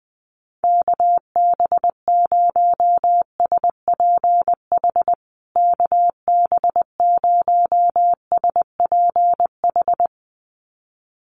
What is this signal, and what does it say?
Morse code, heard as a keyed tone, about 20 wpm, 700 Hz. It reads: KB0SPH KB0SPH
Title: USA Call Signs Repeated 2x 20wpm